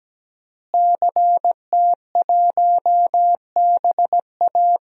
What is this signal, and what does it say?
Morse code, heard as a keyed tone, about 17 wpm, 700 Hz. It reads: CT1BA
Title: DXCC-100 Call Signs - 5 Characters - Repeated 3x - 17wpm